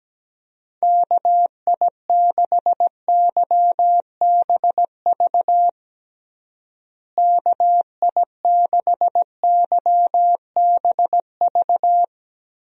Morse code, heard as a keyed tone, about 17 wpm, 700 Hz. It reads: KI6YBV KI6YBV
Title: USA Call Signs - 6 Characters - Repeated 3x - 17wpm